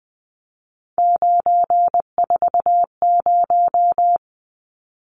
Morse code, heard as a keyed tone, about 20 wpm, 700 Hz. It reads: 940
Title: Numbers - 3-Digit 20wpm